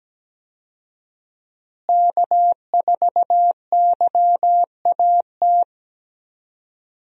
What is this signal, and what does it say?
Morse code, heard as a keyed tone, about 17 wpm, 700 Hz. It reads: K4YAT
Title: USA Call Signs - 5 Characters - Repeated 2x - 17wpm